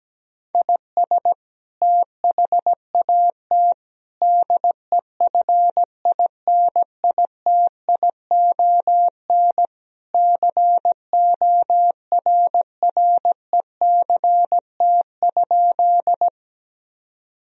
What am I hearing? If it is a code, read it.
Morse code, heard as a keyed tone, about 17 wpm, 700 Hz. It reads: IS THAT DEFINITION CORRECT?
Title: Sentences from Top 1000 Words - Review Part 4 - 17wpm